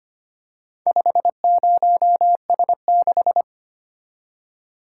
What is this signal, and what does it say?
Morse code, heard as a keyed tone, about 25 wpm, 700 Hz. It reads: 50S6
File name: practice / 25wpm / Four Letter-Number Combinations - 25wpm